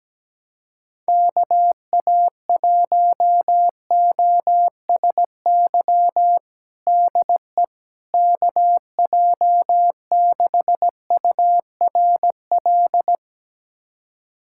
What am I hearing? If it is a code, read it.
Morse code, heard as a keyed tone, about 17 wpm, 700 Hz. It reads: KA1OSY DE KJ6URL
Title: USA Callsign DE Callsign - 6 Characters - 17wpm